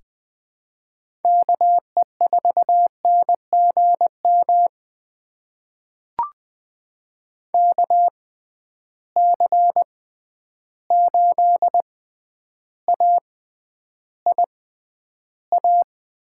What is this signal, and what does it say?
Morse code, heard as a keyed tone, about 20 wpm, spaced as for 10 wpm, 700 Hz. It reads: KE4NGM KC8AIA
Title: USA Call Signs - 6 Characters - 6x Character Spacing - 20wpm